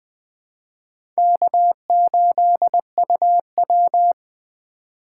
Morse code, heard as a keyed tone, about 21 wpm, 700 Hz. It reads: K8UW
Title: USA Call Signs - 3x Character Spacing - 20wpm